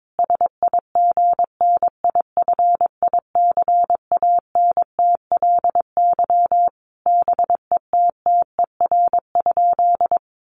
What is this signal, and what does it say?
Morse code, heard as a keyed tone, about 22 wpm, 700 Hz. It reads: SIGNIFICANTLY BETTER?
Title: Sentences from Top 1500 Words 22wpm